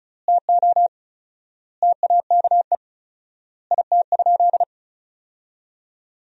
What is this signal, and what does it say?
Morse code, heard as a keyed tone, about 34 wpm, 700 Hz. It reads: TO TAKE IT?
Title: 4x Word-Spacing - Sentences from Top 300 Words - 35wpm